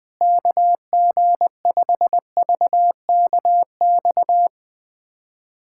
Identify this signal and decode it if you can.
Morse code, heard as a keyed tone, about 20 wpm, 700 Hz. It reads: KG5VKX
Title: USA Callsign DE Callsign - 6 Characters - 20wpm